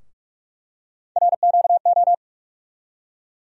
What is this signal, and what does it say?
Morse code, heard as a keyed tone, about 42 wpm, 700 Hz. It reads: RQO
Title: USA Call Sign Postfixes - 3 Letters 45wpm